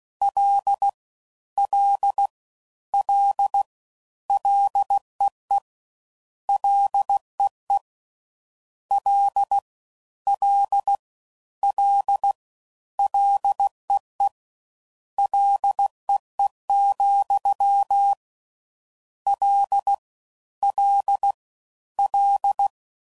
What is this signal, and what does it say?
Morse code, heard as a keyed tone, about 16 wpm, 800 Hz. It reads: L L L LEE LEE L L L LEE LEE, L L L